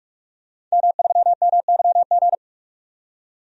Morse code, heard as a keyed tone, about 46 wpm, 700 Hz. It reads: M3MYG
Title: DXCC-100 Call Signs - 5 Characters - 45wpm